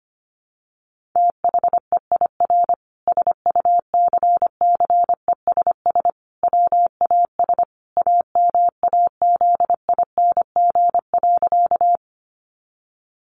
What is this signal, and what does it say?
Morse code, heard as a keyed tone, about 25 wpm, 700 Hz. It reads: THEIR SUCCESS WAS AMAZING.
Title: Sentences from Top 1500 Words 25wpm